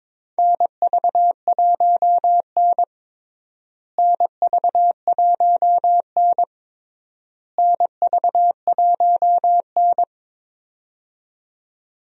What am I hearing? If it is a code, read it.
Morse code, heard as a keyed tone, about 22 wpm, 700 Hz. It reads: NV1N NV1N NV1N
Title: USA Call Signs - 4 Characters - Repeated 3x - 22wpm